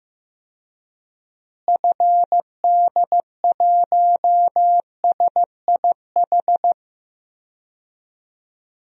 Morse code, heard as a keyed tone, about 15 wpm, 700 Hz. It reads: FD1SIH
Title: DXCC-100 Call Signs - 6 Characters - Repeated 3x - 15wpm